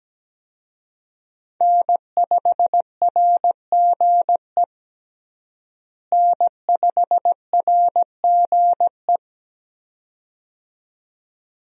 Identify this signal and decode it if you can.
Morse code, heard as a keyed tone, about 17 wpm, 700 Hz. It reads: N5RGE N5RGE
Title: USA Call Signs - 5 Characters - Repeated 2x - 17wpm